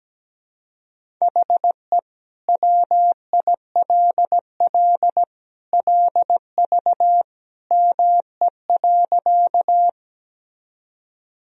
Morse code, heard as a keyed tone, about 17 wpm, 700 Hz. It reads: HE WILL LV ME.
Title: Sentences from Top 400 Words and Abbreviations 17wpm